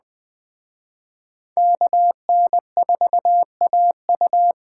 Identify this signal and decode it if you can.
Morse code, heard as a keyed tone, about 20 wpm, 700 Hz. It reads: KN4AU
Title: Speed Racing - USA 2x2 Call Signs 17wpm